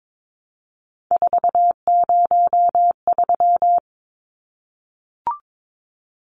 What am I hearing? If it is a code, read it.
Morse code, heard as a keyed tone, about 22 wpm, 700 Hz. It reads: 403
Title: Numbers - 3-Digit 22wpm